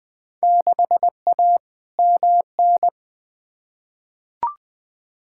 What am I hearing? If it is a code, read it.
Morse code, heard as a keyed tone, about 20 wpm, 700 Hz. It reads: 6A MN
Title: Field Day Exchanges Encore - 20wpm